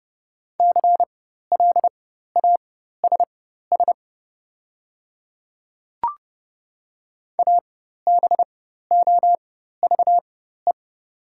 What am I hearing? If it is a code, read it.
Morse code, heard as a keyed tone, about 30 wpm, spaced as for 14 wpm, 700 Hz. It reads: CLASS ABOVE
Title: 4x Character Spacing - Top 500 Words - 30wpm